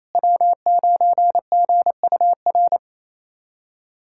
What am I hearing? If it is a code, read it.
Morse code, heard as a keyed tone, about 28 wpm, 700 Hz. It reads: W9GUR
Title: USA Callsign DE Callsign - 5 Characters - 28wpm